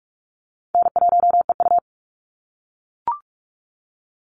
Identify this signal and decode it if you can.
Morse code, heard as a keyed tone, about 44 wpm, 700 Hz. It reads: N1EU
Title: NA SOTA Activator Callsigns 45wpm